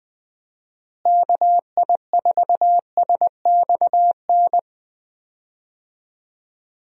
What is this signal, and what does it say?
Morse code, heard as a keyed tone, about 20 wpm, 700 Hz. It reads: KI4SXN
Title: USA Call Signs - 6 Characters - Repeated 3x - 20wpm